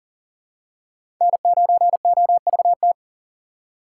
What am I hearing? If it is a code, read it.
Morse code, heard as a keyed tone, about 40 wpm, 700 Hz. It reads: N9OVT
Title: USA Call Signs - 4x Character Spacing - 40wpm